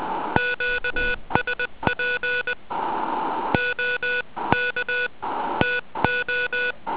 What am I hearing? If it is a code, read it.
Morse code, heard as a keyed tone, about 20 wpm, spaced as for 18 wpm, 1470 Hz. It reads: QSP OK TO
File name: ARCHIV / zvuk1 / DL2DZL